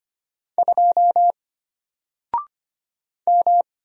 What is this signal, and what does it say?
Morse code, heard as a keyed tone, about 25 wpm, 700 Hz. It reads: 2 M
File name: Single-Letter-Number-25wpm